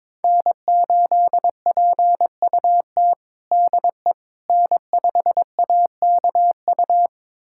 Morse code, heard as a keyed tone, about 22 wpm, 700 Hz. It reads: N8PUT DE N5AKU
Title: USA Callsign DE Callsign - 5 Characters - 22wpm